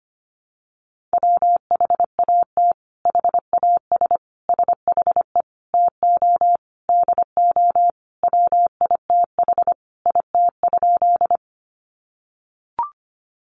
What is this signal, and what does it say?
Morse code, heard as a keyed tone, about 25 wpm, 700 Hz. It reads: WHAT HAS SHE TO DO WITH IT?
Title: Sentences from Top 100 Words 25wpm